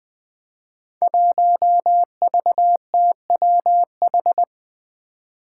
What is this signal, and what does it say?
Morse code, heard as a keyed tone, about 20 wpm, 700 Hz. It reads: 1VTWH
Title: Five Challenging Letter-Number Combinations 20wpm